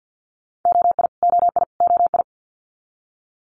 Morse code, heard as a keyed tone, about 50 wpm, 700 Hz. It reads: OI OI OI
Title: Multi-letter Phonemes Repeated 3x 50wpm